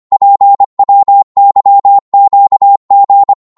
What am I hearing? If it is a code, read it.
Morse code, heard as a keyed tone, about 24 wpm, 800 Hz. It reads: PWYQG
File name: Morse / ltr-25wpm